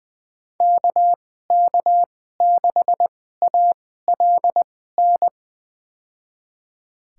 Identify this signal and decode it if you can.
Morse code, heard as a keyed tone, about 20 wpm, spaced as for 14 wpm, 700 Hz. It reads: KK6ALN
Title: USA Call Signs - 6 Characters - 2x Character Spacing - 20wpm